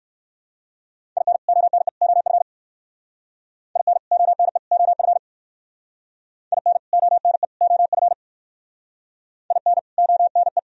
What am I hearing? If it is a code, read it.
Morse code, heard as a keyed tone, about 58 wpm, 700 Hz. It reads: IN ONE OF IN ONE OF IN ONE OF IN ONE